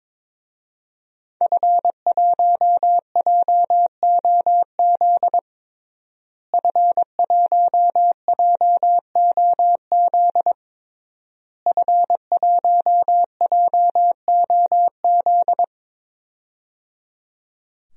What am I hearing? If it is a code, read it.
Morse code, heard as a keyed tone, about 22 wpm, 700 Hz. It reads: F1JOZ F1JOZ F1JOZ
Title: DXCC-100 Call Signs - 5 Characters - Repeated 3x - 22wpm